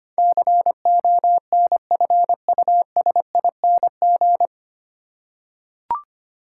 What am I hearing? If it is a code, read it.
Morse code, heard as a keyed tone, about 25 wpm, 700 Hz. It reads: CONFUSING
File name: Nine Letter Words - 25wpm